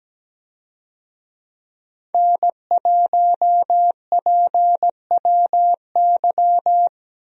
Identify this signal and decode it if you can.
Morse code, heard as a keyed tone, about 17 wpm, 700 Hz. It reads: N1PWY
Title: USA Call Signs - 5 Characters - Repeated 2x - 17wpm